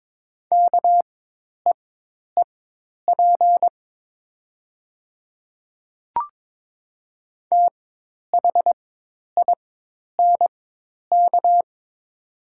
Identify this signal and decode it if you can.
Morse code, heard as a keyed tone, about 22 wpm, spaced as for 10 wpm, 700 Hz. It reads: KEEP THINK
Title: 4x Character Spacing - Top 300 Words - 22wpm